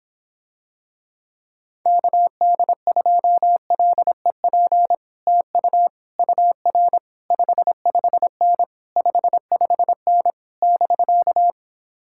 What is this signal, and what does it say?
Morse code, heard as a keyed tone, about 26 wpm, 700 Hz. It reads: KD2LEP TU UR 55N 55N <BK>